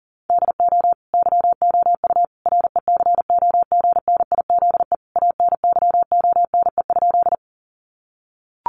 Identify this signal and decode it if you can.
Morse code, heard as a keyed tone, about 40 wpm, 700 Hz. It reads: DO YOU RECOGNIZE ANYONE?
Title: Sentences from Top 1400 Words 40wpm